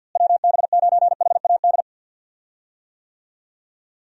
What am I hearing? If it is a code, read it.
Morse code, heard as a keyed tone, about 49 wpm, 700 Hz. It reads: WB9HAD
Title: USA Call Signs - 6 Characters - Repeated 3x - 50wpm